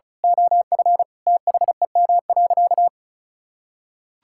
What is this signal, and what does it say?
Morse code, heard as a keyed tone, about 36 wpm, 700 Hz. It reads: OF THEM.
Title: Sentences from Top 1000 Words - Review Part 6 - 35wpm